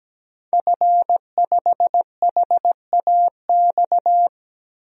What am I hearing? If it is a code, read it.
Morse code, heard as a keyed tone, about 17 wpm, 700 Hz. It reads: F5HAX
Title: DXCC-100 Call Signs - 5 Characters - 17wpm